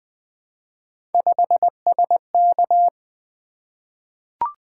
Morse code, heard as a keyed tone, about 20 wpm, 700 Hz. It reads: 5SK
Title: Three Letter-Number Combinations - 20wpm